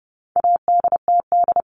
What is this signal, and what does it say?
Morse code, heard as a keyed tone, about 29 wpm, 700 Hz. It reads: ADTD
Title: All Cut Numbers - 4-Digit 30wpm